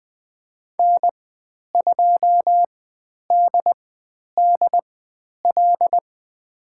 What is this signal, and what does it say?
Morse code, heard as a keyed tone, about 20 wpm, spaced as for 10 wpm, 700 Hz. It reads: N2DDL
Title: 109 Lesson 18g - Letter m - Call Signs 20wpm